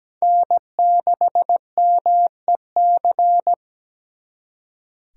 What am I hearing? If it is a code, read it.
Morse code, heard as a keyed tone, about 20 wpm, 700 Hz. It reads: N6MEC
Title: Speed Racing - USA 1x3 Call Signs 17wpm